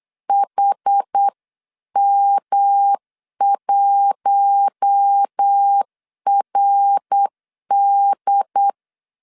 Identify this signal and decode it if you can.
Morse code, heard as a keyed tone, about 8 wpm, 800 Hz. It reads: HM1RD
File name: LEZIONI / 7 GF9Z / 7 G 8wpm 3118